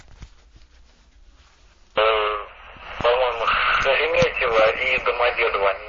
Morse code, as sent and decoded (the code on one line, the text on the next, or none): none